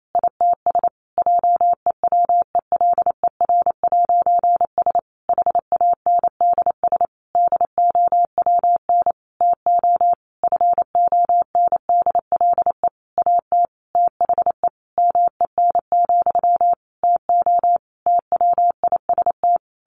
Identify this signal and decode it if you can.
Morse code, heard as a keyed tone, about 28 wpm, 700 Hz. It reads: ITS JEWELER'S HANDS DOWN TO FONDLE AT THE MEN, TO TWIST
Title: A Sound Of Thunder 28wpm